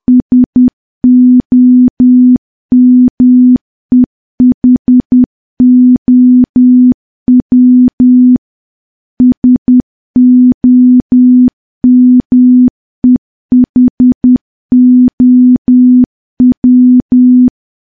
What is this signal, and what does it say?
Morse code, heard as a keyed tone, about 10 wpm, 250 Hz. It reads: SOMEHOW SOMEHOW
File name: morse / morse-training-10WPM